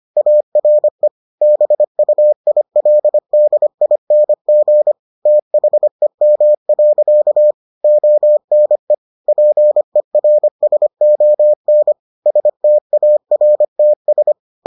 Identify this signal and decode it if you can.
Morse code, heard as a keyed tone, about 25 wpm, 600 Hz. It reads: ARE BUILDING THEM. ONE PERSON STARTS